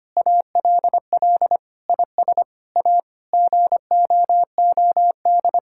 Morse code, heard as a keyed tone, about 25 wpm, 700 Hz. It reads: ALL IS A GOOD
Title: Sentences from Top 1000 Words - Review Part 6 - 25wpm